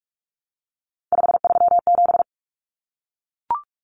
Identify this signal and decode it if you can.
Morse code, heard as a keyed tone, about 44 wpm, 700 Hz. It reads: 537
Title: Numbers - 3-Digit 45wpm